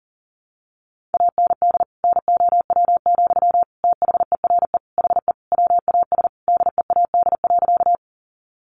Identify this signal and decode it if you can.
Morse code, heard as a keyed tone, about 37 wpm, 700 Hz. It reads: AND NOW, THERE HE WAS DEAD.
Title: Sentences from Top 1500 Words 40wpm